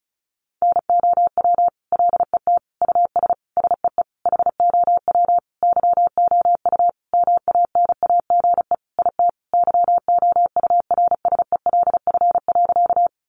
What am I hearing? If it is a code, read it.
Morse code, heard as a keyed tone, about 34 wpm, 700 Hz. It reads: NOW LET US SEE HOW YOU MANAGE IT YOURSELF.
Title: Sentences from Top 1400 Words 35wpm